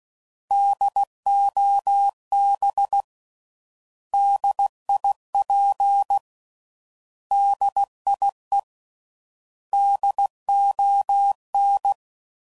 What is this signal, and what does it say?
Morse code, heard as a keyed tone, about 16 wpm, 800 Hz. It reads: DOB DIP DIE DON
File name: Prefixes / By First Letter / Ds